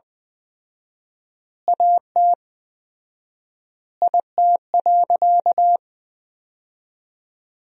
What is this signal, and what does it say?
Morse code, heard as a keyed tone, about 20 wpm, 700 Hz. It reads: AT IT.
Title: 4x Word-Spacing - Sentences from Top 300 Words - 20wpm